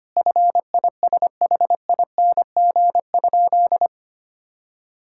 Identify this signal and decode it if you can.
Morse code, heard as a keyed tone, about 25 wpm, 700 Hz. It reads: FISHING?
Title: Sentences from Top 1000 Words - Review Part 5 - 25wpm